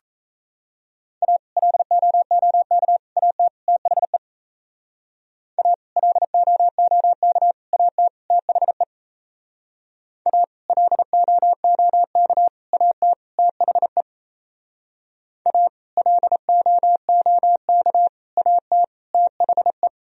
Morse code, heard as a keyed tone, about 35 wpm, 700 Hz. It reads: A LOOK AT THE A LOOK AT THE A LOOK AT THE A LOOK AT THE
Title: Speed-Racing - Sets of 4 Words 28wpm